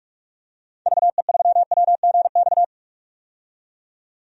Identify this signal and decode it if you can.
Morse code, heard as a keyed tone, about 46 wpm, 700 Hz. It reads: VE3WGX
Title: DXCC-100 Call Signs - 6 Characters - 45wpm